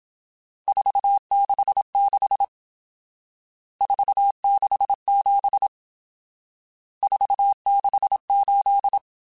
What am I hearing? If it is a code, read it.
Morse code, heard as a keyed tone, about 27 wpm, 800 Hz. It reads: 466 467 468
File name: RSTs / RST-mp3s / Various RST @25